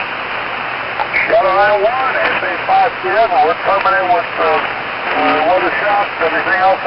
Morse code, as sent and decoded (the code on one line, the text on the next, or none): none